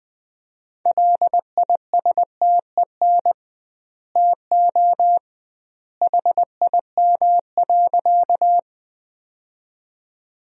Morse code, heard as a keyed tone, about 20 wpm, 700 Hz. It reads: LISTEN TO HIM.
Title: 2x Word-Spacing - Sentences from Top 400 Words - 20wpm